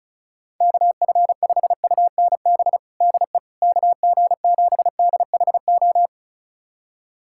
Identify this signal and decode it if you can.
Morse code, heard as a keyed tone, about 36 wpm, 700 Hz. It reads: KF5UNB DE KG7DHO
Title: USA Callsign DE Callsign - 6 Characters - 35wpm